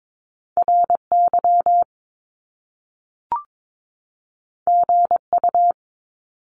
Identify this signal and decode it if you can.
Morse code, heard as a keyed tone, about 22 wpm, 700 Hz. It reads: RY GU
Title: Top Two Letter Combinations in Words 22wpm